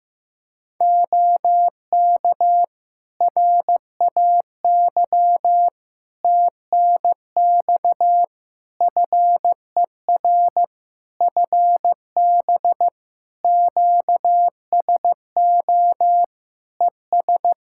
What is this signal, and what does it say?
Morse code, heard as a keyed tone, about 15 wpm, 700 Hz. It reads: OK RAY TNX FER FB QSO ES
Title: Practice QSO - Exchange 4 - Ending - 15wpm